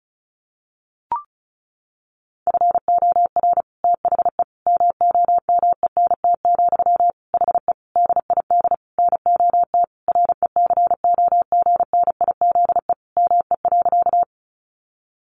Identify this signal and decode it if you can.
Morse code, heard as a keyed tone, about 34 wpm, 700 Hz. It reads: FOR THE MOMENT, HE DID NOT RECOGNIZE ME.
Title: Sentences from Top 1400 Words 35wpm